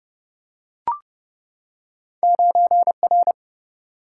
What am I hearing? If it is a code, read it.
Morse code, heard as a keyed tone, about 30 wpm, 700 Hz. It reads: 9R
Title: Single-Letter Single-Number 30wpm